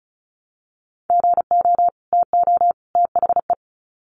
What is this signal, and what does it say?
Morse code, heard as a keyed tone, about 34 wpm, 700 Hz. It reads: GO TO THE
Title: Sets of 3 Words 35wpm